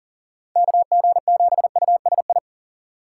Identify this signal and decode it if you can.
Morse code, heard as a keyed tone, about 40 wpm, 700 Hz. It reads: KG7USI
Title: USA Call Signs - 6 Characters - 3x Character Spacing - 40wpm